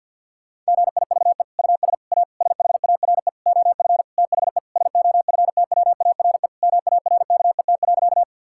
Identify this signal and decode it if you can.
Morse code, heard as a keyed tone, about 48 wpm, 700 Hz. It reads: GIVE US A SHARE OF THE SOFTWARE MARKET.
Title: Sentences from Top 1000 Words - Review Part 6 - 50wpm